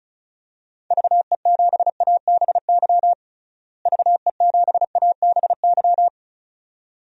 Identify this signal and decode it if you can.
Morse code, heard as a keyed tone, about 36 wpm, 700 Hz. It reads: VE7ABY VE7ABY
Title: DXCC-100 Call Signs - 6 Characters - Repeated 2x - 35wpm